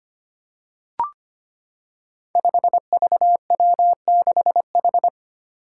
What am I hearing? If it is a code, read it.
Morse code, heard as a keyed tone, about 25 wpm, 700 Hz. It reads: 5VW6H
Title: Five Challenging Letter-Number Combinations 25wpm